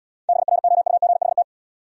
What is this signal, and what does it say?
Morse code, heard as a keyed tone, about 73 wpm, 700 Hz. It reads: BROUGHT